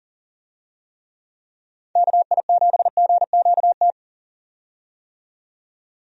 Morse code, heard as a keyed tone, about 40 wpm, 700 Hz. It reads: KI7GQT
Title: USA Call Signs - 6 Characters - 40wpm